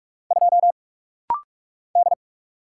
Morse code, heard as a keyed tone, about 44 wpm, 690 Hz. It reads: 2 D
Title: Single Letter-Number - Rapid-Fire 45wpm